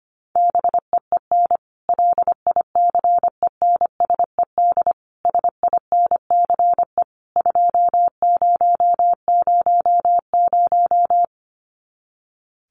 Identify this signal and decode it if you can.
Morse code, heard as a keyed tone, about 25 wpm, 700 Hz. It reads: BEEN LICENSED SINCE 2000
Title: Common QSO Phrases 25wpm